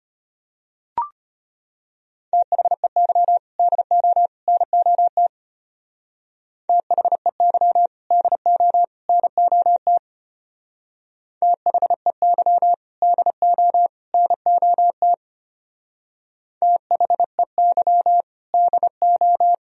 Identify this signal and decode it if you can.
Morse code, heard as a keyed tone, about 31 wpm, 700 Hz. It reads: THEY DO NOT THEY DO NOT THEY DO NOT THEY DO
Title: Speed-Racing - Sets of 3 Words 25wpm